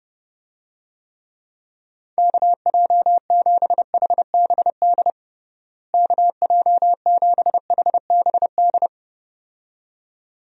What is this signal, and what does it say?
Morse code, heard as a keyed tone, about 30 wpm, 700 Hz. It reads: KJ7HBD KJ7HBD
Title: USA Call Signs - 6 Characters - Repeated 2x - 30wpm